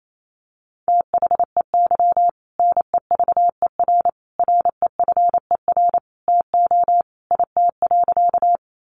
Morse code, heard as a keyed tone, about 28 wpm, 700 Hz. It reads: THEY NEVER REFER TO IT.
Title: Sentences from Top 1000 Words 28wpm